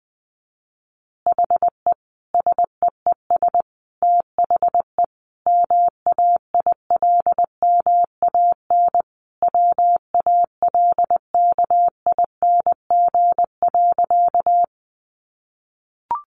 Morse code, heard as a keyed tone, about 19 wpm, 700 Hz. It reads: HE SEES THE MAILMAN WALKING.
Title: Easy Phrases 20wpm